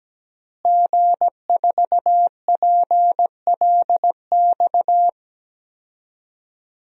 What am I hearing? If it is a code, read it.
Morse code, heard as a keyed tone, about 17 wpm, 700 Hz. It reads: G4PLX
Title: DXCC-100 Call Signs - 5 Characters - Repeated 3x - 17wpm